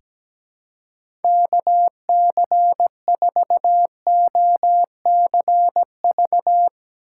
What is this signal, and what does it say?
Morse code, heard as a keyed tone, about 17 wpm, 700 Hz. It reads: KC4OCV
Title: USA Call Signs - 6 Characters - 2x Character Spacing - 17wpm